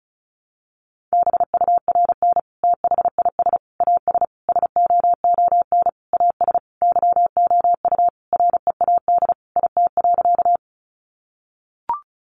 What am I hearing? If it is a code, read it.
Morse code, heard as a keyed tone, about 34 wpm, 700 Hz. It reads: BURN THIS AS SOON AS YOU READ IT.
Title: Sentences from Top 1300 Words 35wpm